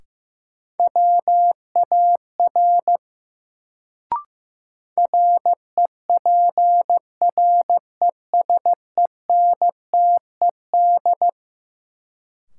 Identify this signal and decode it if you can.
Morse code, heard as a keyed tone, about 15 wpm, 700 Hz. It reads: WAR REPRESENTED
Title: Sets of 5 Words - New Word Drill - 15wpm